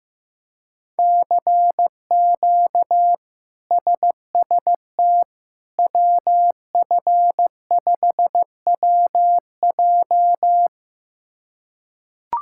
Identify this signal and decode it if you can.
Morse code, heard as a keyed tone, about 15 wpm, 700 Hz. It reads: CQ SST WF5WJ
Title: K1USN SST QSO Practice - 15wpm